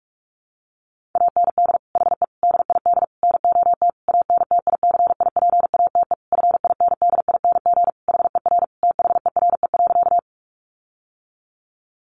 Morse code, heard as a keyed tone, about 44 wpm, 700 Hz. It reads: AND HE DID NOT ANTICIPATE FINDING HER THERE.
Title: Sentences from Top 1500 Words 45wpm